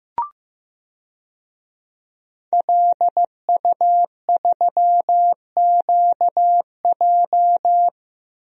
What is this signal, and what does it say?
Morse code, heard as a keyed tone, about 15 wpm, 700 Hz. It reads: LU3QJ
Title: DXCC-100 Call Signs - 5 Characters - 15wpm